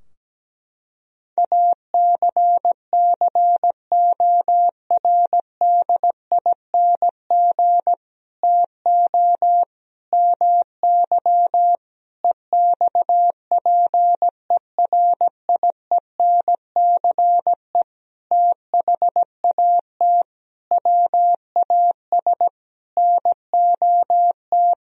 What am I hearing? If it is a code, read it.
Morse code, heard as a keyed tone, about 17 wpm, 700 Hz. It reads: ACCORDING TO MY EXPERIENCE THAT WAS NOT
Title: Sentences from Top 1000 Words - Review Part 4 - 17wpm